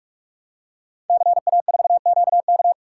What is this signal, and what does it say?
Morse code, heard as a keyed tone, about 44 wpm, 700 Hz. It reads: CA4QK
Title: Code Groups - 5 Letter:Numbers - 45wpm